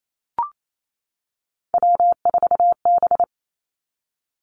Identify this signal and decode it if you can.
Morse code, heard as a keyed tone, about 28 wpm, 700 Hz. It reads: W4B
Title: Three Challenging Letter-Number Combinations 28wpm